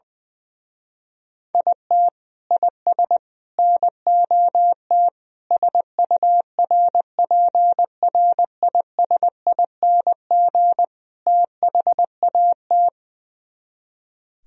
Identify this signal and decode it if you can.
Morse code, heard as a keyed tone, about 20 wpm, 700 Hz. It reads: IT IS NOT SURPRISING THAT
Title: Speed-Racing - Sets of 5 Words 20wpm